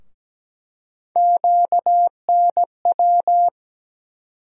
Morse code, heard as a keyed tone, about 17 wpm, 700 Hz. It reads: QNW
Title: USA Call Sign Postfixes - 3 Letters 17wpm